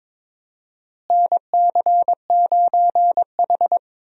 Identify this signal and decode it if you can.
Morse code, heard as a keyed tone, about 22 wpm, 700 Hz. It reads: NC9H
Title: USA Call Signs - 4 Characters - Repeated 3x - 22wpm